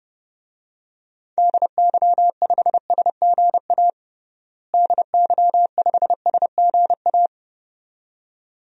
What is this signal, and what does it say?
Morse code, heard as a keyed tone, about 30 wpm, 700 Hz. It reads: DY5SGA DY5SGA
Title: DXCC-100 Call Signs - 6 Characters - Repeated 2x - 30wpm